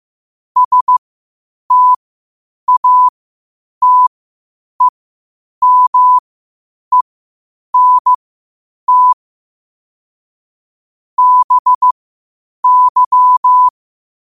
Morse code, heard as a keyed tone, about 15 wpm, spaced as for 8 wpm, 1000 Hz. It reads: STATEMENT BY
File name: qotdfiles / 8wpm